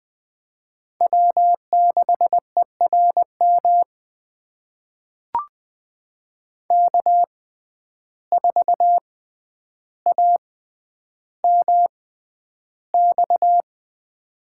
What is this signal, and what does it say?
Morse code, heard as a keyed tone, about 20 wpm, spaced as for 10 wpm, 700 Hz. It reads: W6ERM K4AMX
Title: USA Call Signs - 5 Characters - 6x Character Spacing - 20wpm